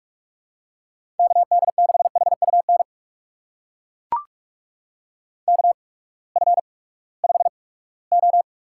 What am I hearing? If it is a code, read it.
Morse code, heard as a keyed tone, about 42 wpm, spaced as for 22 wpm, 700 Hz. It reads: KD6HUN KF5O
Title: USA Call Signs - 6 Characters - 8x Character Spacing - 45wpm